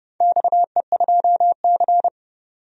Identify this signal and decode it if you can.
Morse code, heard as a keyed tone, about 32 wpm, 700 Hz. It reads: XE2C
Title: DXCC-100 Call Signs - 4 Characters - 30wpm